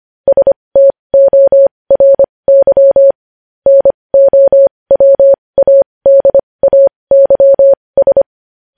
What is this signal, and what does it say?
Morse code, heard as a keyed tone, about 25 wpm, spaced as for 20 wpm, 550 Hz. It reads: STORY NOWADAYS
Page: news-headlines-in-morse-code-at-20-wpm